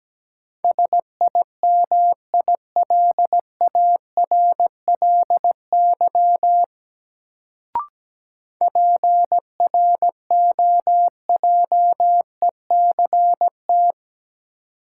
Morse code, heard as a keyed tone, about 17 wpm, 700 Hz. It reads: SIMILARLY PROJECT
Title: Sentences from Top 1400 Words - New Word Drill - 17wpm